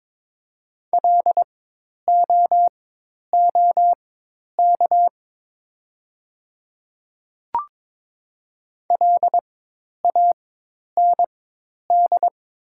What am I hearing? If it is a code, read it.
Morse code, heard as a keyed tone, about 22 wpm, spaced as for 10 wpm, 700 Hz. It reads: LOOK LAND
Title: 4x Character Spacing - Top 200 Words - 22wpm